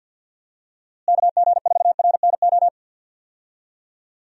Morse code, heard as a keyed tone, about 51 wpm, 700 Hz. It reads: KG4RNO
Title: USA Call Signs - 6 Characters - Repeated 3x - 50wpm